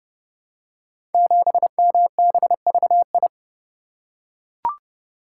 Morse code, heard as a keyed tone, about 30 wpm, 700 Hz. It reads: 7MBVI
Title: Code Groups - 5 Letter:Numbers - 30wpm